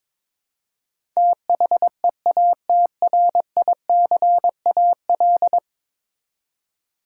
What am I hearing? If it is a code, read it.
Morse code, heard as a keyed tone, about 22 wpm, 700 Hz. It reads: THEATRICAL
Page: morse-code-ninja-practice-22wpm